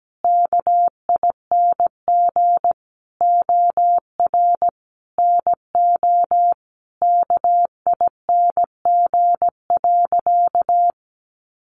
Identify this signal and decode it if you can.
Morse code, heard as a keyed tone, about 17 wpm, 700 Hz. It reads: KING OR NO KING.
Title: Sentences from Top 600 Words 17wpm